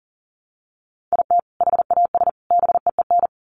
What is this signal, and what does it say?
Morse code, heard as a keyed tone, about 40 wpm, 700 Hz. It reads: IT HAS BEEN
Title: Sets of 3 Words 40wpm